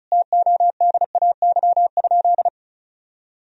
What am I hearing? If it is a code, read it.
Morse code, heard as a keyed tone, about 34 wpm, 700 Hz. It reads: TODAY?
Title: Sentences from Top 1000 Words - Review Part 2 - 35wpm